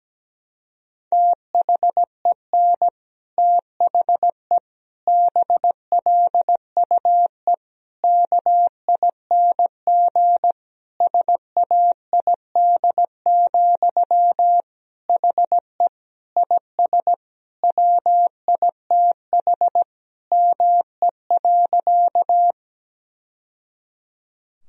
Morse code, heard as a keyed tone, about 17 wpm, 700 Hz. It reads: THEN THE BLUE KING SAID, HE IS WITH ME.
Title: Sentences from Top 1000 Words - Review Part 3 - 17wpm